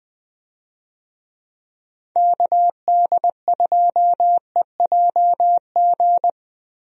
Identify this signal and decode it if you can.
Morse code, heard as a keyed tone, about 20 wpm, 700 Hz. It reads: KD2EJG
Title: USA Call Signs - 6 Characters - Repeated 3x - 20wpm